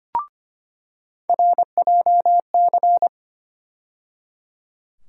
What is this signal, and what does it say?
Morse code, heard as a keyed tone, about 25 wpm, 700 Hz. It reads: RJC
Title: USA Call Sign Postfixes - 3 Letters 25wpm